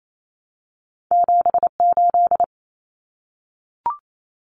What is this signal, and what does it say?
Morse code, heard as a keyed tone, about 28 wpm, 700 Hz. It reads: 7 8
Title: Common Cut Numbers - 2-Digit 28wpm